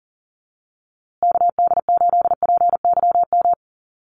Morse code, heard as a keyed tone, about 38 wpm, 700 Hz. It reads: KD8PYM
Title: USA Call Signs Repeated 2x 40wpm